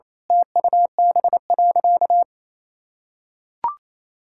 Morse code, heard as a keyed tone, about 28 wpm, 700 Hz. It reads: TUB.
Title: Pangram Challenge v2 - 28wpm